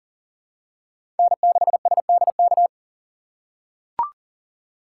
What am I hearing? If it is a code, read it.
Morse code, heard as a keyed tone, about 40 wpm, 700 Hz. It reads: N6SDK
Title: USA Call Signs - 2x Character Spacing - 40wpm